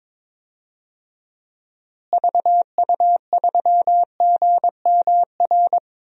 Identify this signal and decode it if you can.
Morse code, heard as a keyed tone, about 22 wpm, 700 Hz. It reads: VU3GMR
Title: DXCC-100 Call Signs - 6 Characters - Repeated 2x - 22wpm